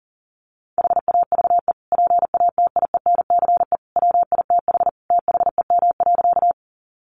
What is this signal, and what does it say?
Morse code, heard as a keyed tone, about 40 wpm, 700 Hz. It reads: HAVE PATIENCE WITH THEM.
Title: Sentences from Top 1300 Words 40wpm